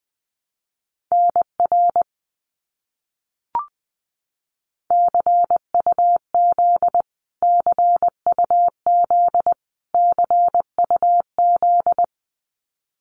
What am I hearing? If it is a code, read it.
Morse code, heard as a keyed tone, about 19 wpm, 700 Hz. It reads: NR CUZ CUZ CUZ
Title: Abbreviations Repeated 3x - 20wpm